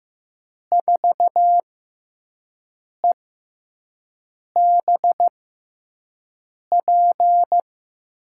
Morse code, heard as a keyed tone, about 15 wpm, spaced as for 5 wpm, 700 Hz. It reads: 4EBP
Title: USA Call Signs - 6 Characters - 6x Character Spacing - 15wpm